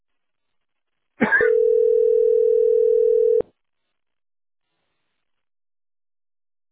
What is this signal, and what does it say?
Ringback tone (North America), 440 + 480 Hz.